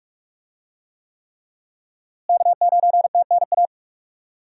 Morse code, heard as a keyed tone, about 46 wpm, 700 Hz. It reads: K9TNA
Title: USA Call Signs - 5 Characters - Repeated 3x - 45wpm